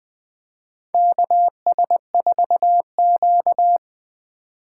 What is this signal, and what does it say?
Morse code, heard as a keyed tone, about 21 wpm, 700 Hz. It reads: KS4Q
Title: USA Call Signs - 4 Characters - 3x Character Spacing - 20wpm